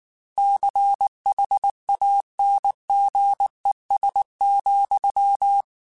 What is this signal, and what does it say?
Morse code, heard as a keyed tone, about 19 wpm, 800 Hz. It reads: CHANGES,
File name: Technical Subjects / Technical Subjects-mp3s / Reactance @18